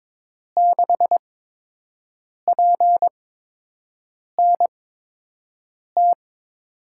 Morse code, heard as a keyed tone, about 22 wpm, spaced as for 6 wpm, 700 Hz. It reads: 6PNT
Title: USA Call Signs - 8x Character Spacing - 22wpm